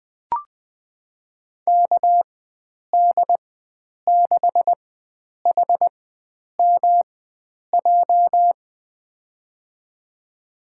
Morse code, heard as a keyed tone, about 21 wpm, spaced as for 9 wpm, 700 Hz. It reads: KD6HMJ
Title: USA Call Signs - 6 Characters - 4x Character Spacing - 20wpm